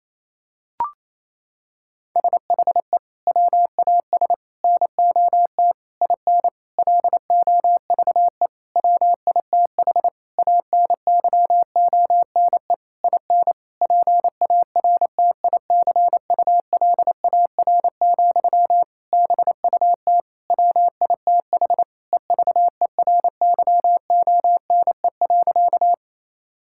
Morse code, heard as a keyed tone, about 28 wpm, 700 Hz. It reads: SHE WAS NOT IN LOVE WITH ANYONE IN PARTICULAR, BUT WITH EVERYONE.
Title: Sentences from Top 1000 Words - Review Part 5 - 28wpm